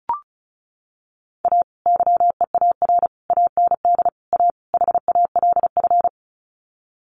Morse code, heard as a keyed tone, about 34 wpm, 700 Hz. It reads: A YEAR AND A HALF
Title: Sets of 5 Words 35wpm